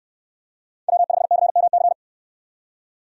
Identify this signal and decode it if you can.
Morse code, heard as a keyed tone, about 69 wpm, 700 Hz. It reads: W5CMC